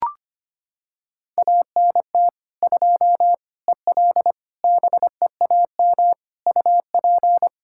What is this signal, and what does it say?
Morse code, heard as a keyed tone, about 25 wpm, 700 Hz. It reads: ANT 2 EL BEAM UP
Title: Practice QSO - Exchange 2 - Info - 25wpm